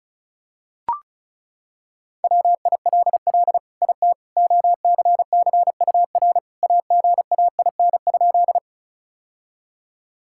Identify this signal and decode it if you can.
Morse code, heard as a keyed tone, about 36 wpm, 700 Hz. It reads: WILL IT OCCUR AGAIN?
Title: Sentences from Top 1000 Words - Review Part 6 - 35wpm